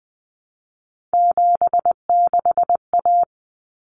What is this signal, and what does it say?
Morse code, heard as a keyed tone, about 20 wpm, 700 Hz. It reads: 76A
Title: Common Cut Numbers - 3-Digit 20wpm